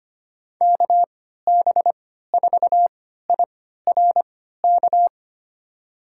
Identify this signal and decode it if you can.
Morse code, heard as a keyed tone, about 25 wpm, spaced as for 14 wpm, 690 Hz. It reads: KB4IRK
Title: USA Call Signs - 6 Characters - 3x Character Spacing - 25wpm